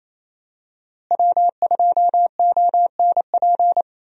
Morse code, heard as a keyed tone, about 28 wpm, 700 Hz. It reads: W2ONP
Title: USA Call Signs - 5 Characters - Repeated 3x - 28wpm